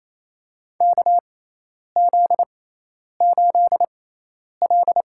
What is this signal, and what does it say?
Morse code, heard as a keyed tone, about 27 wpm, spaced as for 10 wpm, 700 Hz. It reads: KZ8L
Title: USA Call Signs - 4 Characters - 6x Character Spacing - 28wpm